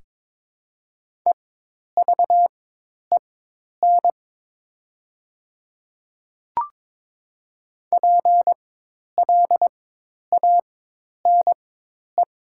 Morse code, heard as a keyed tone, about 22 wpm, spaced as for 10 wpm, 700 Hz. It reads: EVEN PLANE
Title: 4x Character Spacing - Top 500 Words - 22wpm